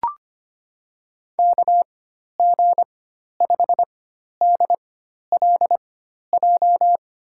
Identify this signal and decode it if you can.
Morse code, heard as a keyed tone, about 25 wpm, spaced as for 12 wpm, 700 Hz. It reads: KG5DLJ
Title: USA Call Signs - 6 Characters - 4x Character Spacing - 25wpm